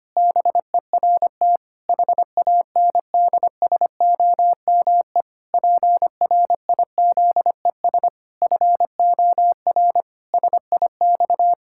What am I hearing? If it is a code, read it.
Morse code, heard as a keyed tone, about 25 wpm, 700 Hz. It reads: BERT HANDSOME PRIZES FOR SIX
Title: Pangram Challenge v2 - 25wpm